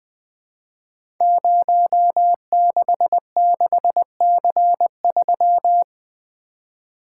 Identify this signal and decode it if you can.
Morse code, heard as a keyed tone, about 20 wpm, 700 Hz. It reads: 066C3
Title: Code Groups - 5 Letter:Numbers - 20wpm